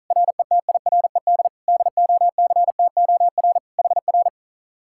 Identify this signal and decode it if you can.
Morse code, heard as a keyed tone, about 43 wpm, 700 Hz. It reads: RETIRED DOCTOR HR
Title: Speed-Racing Common QSO Phrases - 35wpm